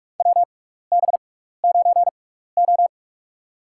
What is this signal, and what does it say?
Morse code, heard as a keyed tone, about 46 wpm, spaced as for 16 wpm, 700 Hz. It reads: WB9O
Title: USA Call Signs - 4 Characters - 6x Character Spacing - 45wpm